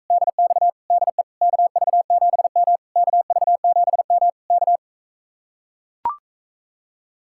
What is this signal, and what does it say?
Morse code, heard as a keyed tone, about 44 wpm, 700 Hz. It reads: DX DE KV7M KV7M K